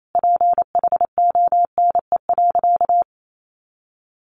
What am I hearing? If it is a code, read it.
Morse code, heard as a keyed tone, about 28 wpm, 700 Hz. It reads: PHONE.
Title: Sentences from Top 1100 Words 28wpm